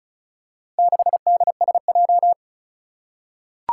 Morse code, heard as a keyed tone, about 35 wpm, 700 Hz. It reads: 6DSJ
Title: Four Challenging Letter-Number Combinations 35wpm